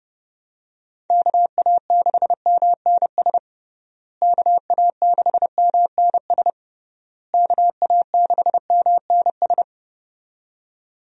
Morse code, heard as a keyed tone, about 32 wpm, 700 Hz. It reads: KA6MNS KA6MNS KA6MNS
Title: USA Call Signs - 6 Characters - Repeated 3x - 30wpm